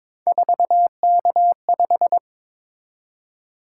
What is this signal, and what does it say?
Morse code, heard as a keyed tone, about 27 wpm, 700 Hz. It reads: 4K5
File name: Speed-Racing - Three Letter-Number Combinations - 22wpm